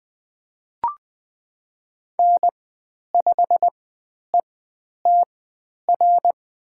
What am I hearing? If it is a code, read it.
Morse code, heard as a keyed tone, about 20 wpm, spaced as for 10 wpm, 700 Hz. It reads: N5ETR
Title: 095 Lesson 16g - Letter c - Call Signs 20wpm